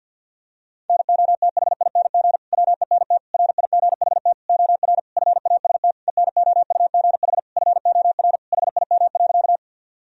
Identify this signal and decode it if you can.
Morse code, heard as a keyed tone, about 52 wpm, 700 Hz. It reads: NOTHING WENT RIGHT OR FAST ENOUGH FOR HIM.
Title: Sentences from Top 1000 Words - Review Part 2 - 50wpm